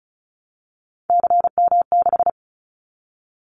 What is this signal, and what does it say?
Morse code, heard as a keyed tone, about 36 wpm, 700 Hz. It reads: CM6
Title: Callsign Prefixes for Top 100 DXCC - 35wpm